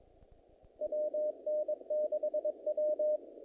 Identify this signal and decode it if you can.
Morse code, heard as a keyed tone, about 22 wpm, 610 Hz. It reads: WN6W
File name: contests / naqp-cw-k5tr-op-k5pi- -2000